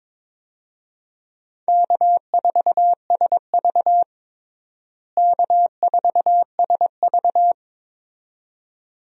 Morse code, heard as a keyed tone, about 22 wpm, 700 Hz. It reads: K4SV K4SV
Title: USA Call Signs - 4 Characters - Repeated 2x - 22wpm